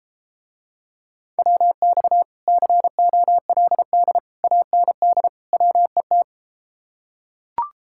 Morse code, heard as a keyed tone, about 33 wpm, 700 Hz. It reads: WX COLD AND WET